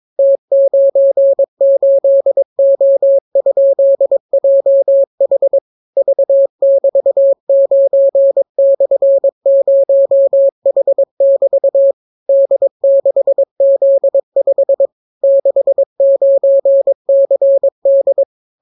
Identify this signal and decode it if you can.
Morse code, heard as a keyed tone, about 22 wpm, 550 Hz. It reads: T98O?JH V<BT>9/0H<BT> D6Z5 69CD